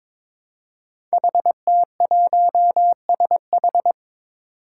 Morse code, heard as a keyed tone, about 23 wpm, 700 Hz. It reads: HT1SH
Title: Five Challenging Letter-Number Combinations 22wpm